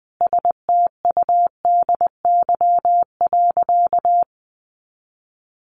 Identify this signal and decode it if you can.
Morse code, heard as a keyed tone, about 20 wpm, 700 Hz. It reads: STUDY.
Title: Additional Easy Phrases 20wpm